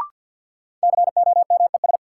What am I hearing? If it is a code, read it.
Morse code, heard as a keyed tone, about 51 wpm, 700 Hz. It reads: COMES
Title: Sentences from Top 1000 Words - Review Part 5 - New Word Drill - 50wpm